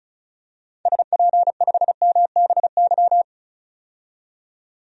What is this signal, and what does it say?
Morse code, heard as a keyed tone, about 36 wpm, 700 Hz. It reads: SP5MBY
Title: DXCC-100 Call Signs - 6 Characters - 35wpm